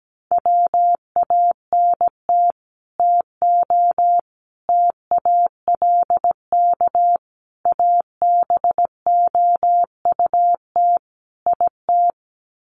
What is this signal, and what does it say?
Morse code, heard as a keyed tone, about 17 wpm, 700 Hz. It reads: WANT TO TALK ABOUT IT
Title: Sets of 5 Words Encore 17wpm